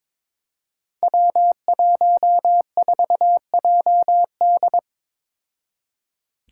Morse code, heard as a keyed tone, about 22 wpm, 700 Hz. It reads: W14JD
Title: Five Challenging Letter-Number Combinations 22wpm